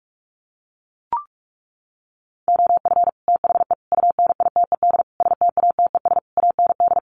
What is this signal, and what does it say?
Morse code, heard as a keyed tone, about 44 wpm, 700 Hz. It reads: OF THE UNITED STATES AND
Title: Sets of 5 Words 45wpm